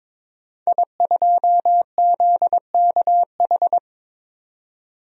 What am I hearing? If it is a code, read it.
Morse code, heard as a keyed tone, about 22 wpm, 700 Hz. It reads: I2ZKH
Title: DXCC-100 Call Signs - 5 Characters - 22wpm